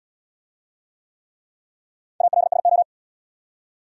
Morse code, heard as a keyed tone, about 79 wpm, 700 Hz. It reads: ALSO